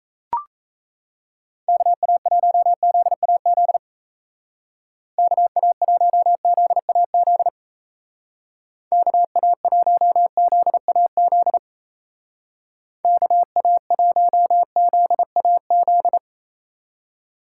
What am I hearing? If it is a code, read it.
Morse code, heard as a keyed tone, about 35 wpm, 700 Hz. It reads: KA1ZAZ KA1ZAZ KA1ZAZ KA1ZAZ
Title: Speed Racing - USA Call Signs 28wpm